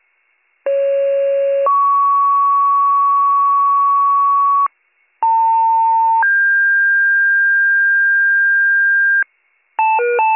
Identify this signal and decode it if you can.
Two-tone page, heard as a one-second tone, then a three-second tone, about 550 then 1120 Hz, about 900 then 1600 Hz, with noise. It is heard from 0.7 to 9.2 s.